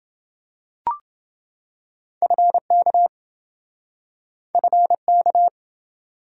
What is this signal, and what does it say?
Morse code, heard as a keyed tone, about 27 wpm, 700 Hz. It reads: FK FK